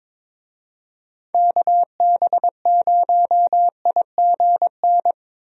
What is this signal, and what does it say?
Morse code, heard as a keyed tone, about 22 wpm, 700 Hz. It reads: KB0IGN
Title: USA Call Signs - 6 Characters - 22wpm